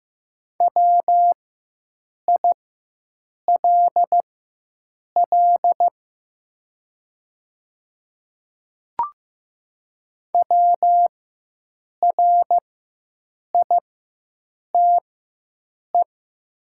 Morse code, heard as a keyed tone, about 15 wpm, spaced as for 7 wpm, 700 Hz. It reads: WILL WRITE
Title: 4x Character Spacing - Top 100 Words - 15wpm